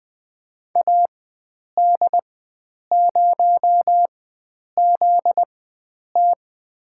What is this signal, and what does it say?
Morse code, heard as a keyed tone, about 20 wpm, spaced as for 9 wpm, 700 Hz. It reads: AD0ZT
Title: USA Call Signs - 4x Character Spacing - 20wpm